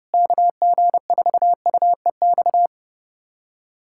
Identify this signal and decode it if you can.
Morse code, heard as a keyed tone, about 28 wpm, 700 Hz. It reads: KG4UEX
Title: USA Callsign DE Callsign - 6 Characters - 30wpm